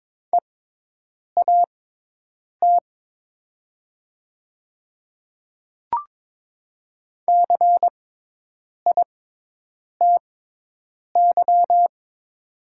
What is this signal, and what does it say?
Morse code, heard as a keyed tone, about 22 wpm, spaced as for 8 wpm, 700 Hz. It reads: EAT CITY
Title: 6x Character Spacing - Top 300 Words - 22wpm